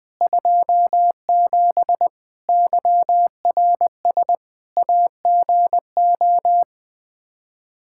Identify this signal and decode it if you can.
Morse code, heard as a keyed tone, about 20 wpm, 700 Hz. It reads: 27 YRS AGO
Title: Practice QSO - Exchange 3 - More Info - 20wpm